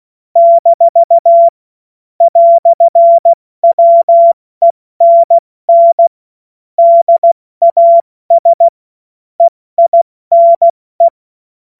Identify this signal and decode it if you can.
Morse code, heard as a keyed tone, about 16 wpm, 680 Hz. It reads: - "WENN DAS EINE